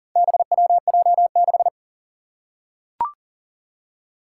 Morse code, heard as a keyed tone, about 40 wpm, 700 Hz. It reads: BWJ6
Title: Four Challenging Letter-Number Combinations 40wpm